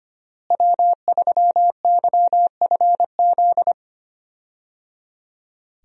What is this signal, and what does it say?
Morse code, heard as a keyed tone, about 25 wpm, 700 Hz. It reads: W3YFZ
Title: USA Call Signs - 5 Characters - Repeated 3x - 25wpm